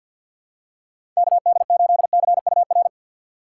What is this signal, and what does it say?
Morse code, heard as a keyed tone, about 53 wpm, 700 Hz. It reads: KD8CUR
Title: USA Call Signs - 6 Characters - 8x Character Spacing - 50wpm